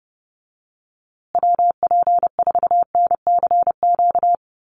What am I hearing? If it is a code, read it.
Morse code, heard as a keyed tone, about 30 wpm, 700 Hz. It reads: WP4NCQ
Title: USA Call Signs Repeated 2x 30wpm